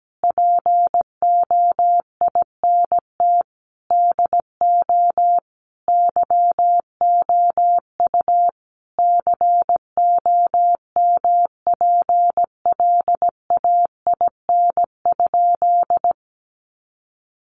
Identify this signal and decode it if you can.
Morse code, heard as a keyed tone, about 17 wpm, 700 Hz. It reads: POINT DO YOU COMPLAIN?
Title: Sentences from Top 1300 Words 17wpm